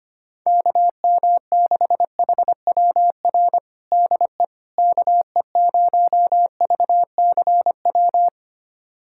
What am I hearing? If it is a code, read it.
Morse code, heard as a keyed tone, about 25 wpm, 700 Hz. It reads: KM6HWR DE KE0VCW
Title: USA Callsign DE Callsign - 6 Characters - 25wpm